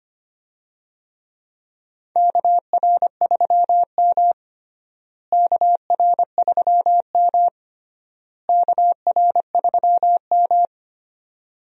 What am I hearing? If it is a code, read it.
Morse code, heard as a keyed tone, about 25 wpm, 700 Hz. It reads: KR3M KR3M KR3M
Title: USA Call Signs - 4 Characters - Repeated 3x - 25wpm